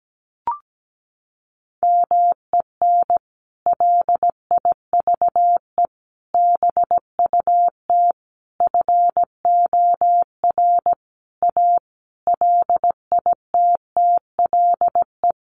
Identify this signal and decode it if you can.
Morse code, heard as a keyed tone, about 17 wpm, 700 Hz. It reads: MEN LIVE BUT FOR A LITTLE
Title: Sentences from Top 1400 Words 17wpm